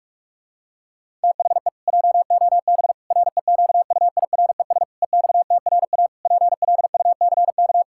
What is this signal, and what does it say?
Morse code, heard as a keyed tone, about 46 wpm, 700 Hz. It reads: THE JOB REQUIRES EXTRA PLUCK
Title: Pangram Challenge v2 - 45wpm